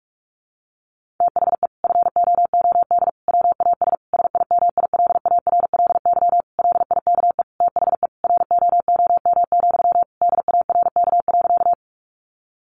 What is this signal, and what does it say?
Morse code, heard as a keyed tone, about 44 wpm, 700 Hz. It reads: THE FOOD WAS SIMILARLY LIKE THE ROOM, DARK.
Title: Sentences from Top 1400 Words 45wpm